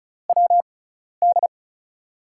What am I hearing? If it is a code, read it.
Morse code, heard as a keyed tone, about 36 wpm, spaced as for 12 wpm, 700 Hz. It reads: WD
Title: USA Call Signs - 6 Characters - 6x Character Spacing - 35wpm